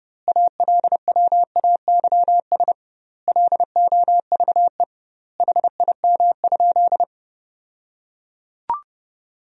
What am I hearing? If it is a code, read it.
Morse code, heard as a keyed tone, about 30 wpm, 700 Hz. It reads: ALWAYS LOVE HIM?
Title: 2x Word-Spacing - Sentences from Top 400 Words - 30wpm